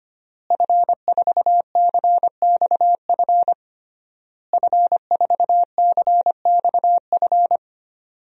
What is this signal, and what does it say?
Morse code, heard as a keyed tone, about 25 wpm, 700 Hz. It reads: F4CXF F4CXF
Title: DXCC-100 Call Signs - 5 Characters - Repeated 3x - 25wpm